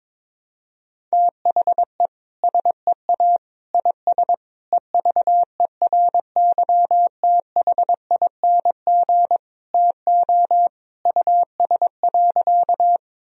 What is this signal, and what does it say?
Morse code, heard as a keyed tone, about 21 wpm, 700 Hz. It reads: THE SEA IS EVERYTHING TO US.
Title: Sentences from Top 1000 Words - Review Part 1 - 22wpm